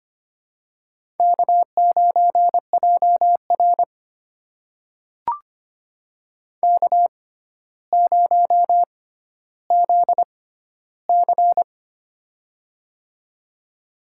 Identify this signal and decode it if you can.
Morse code, heard as a keyed tone, about 26 wpm, spaced as for 13 wpm, 700 Hz. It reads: K9JR K0ZC
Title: USA Call Signs - 4 Characters - 6x Character Spacing - 25wpm